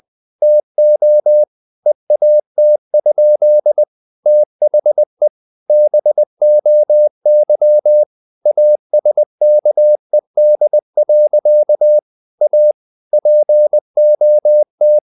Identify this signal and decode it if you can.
Morse code, heard as a keyed tone, about 20 wpm, 600 Hz. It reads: TO EAT? THE BOY ASKED. A POT
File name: the old man and the sea / the old man and the sea part 1 20wpm